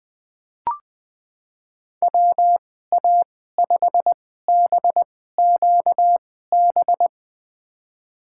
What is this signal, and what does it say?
Morse code, heard as a keyed tone, about 21 wpm, spaced as for 15 wpm, 700 Hz. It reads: WA5BQB
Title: USA Call Signs - 6 Characters - 2x Character Spacing - 20wpm